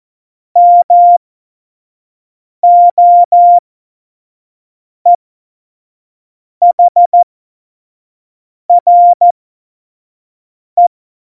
Morse code, heard as a keyed tone, about 14 wpm, spaced as for 5 wpm, 700 Hz. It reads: MOEHRE
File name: Morsekurs / Witze / Witz 07 April 2020, 25 F70